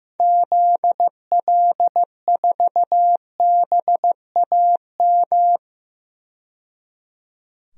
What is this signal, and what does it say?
Morse code, heard as a keyed tone, about 15 wpm, 700 Hz. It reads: ZL4BAM
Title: DXCC-100 Call Signs - 6 Characters - 15wpm